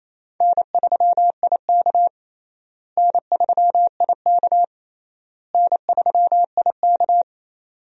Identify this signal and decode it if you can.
Morse code, heard as a keyed tone, about 27 wpm, 700 Hz. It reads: N3IK N3IK N3IK
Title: USA Call Signs - 4 Characters - Repeated 3x - 28wpm